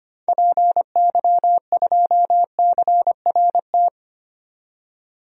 Morse code, heard as a keyed tone, about 25 wpm, 700 Hz. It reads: PY2CRT
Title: DXCC-100 Call Signs - 6 Characters - 25wpm